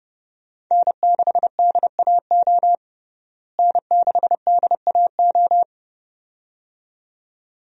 Morse code, heard as a keyed tone, about 32 wpm, 700 Hz. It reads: N6DAO N6DAO
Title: USA Call Signs - 5 Characters - Repeated 2x - 30wpm